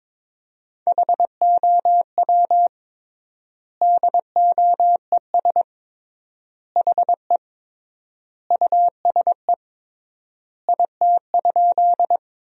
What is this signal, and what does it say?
Morse code, heard as a keyed tone, about 21 wpm, 700 Hz. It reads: HOW DOES HE USE IT?
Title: 3x Word-Spacing - Sentences from Top 200 Words - 22wpm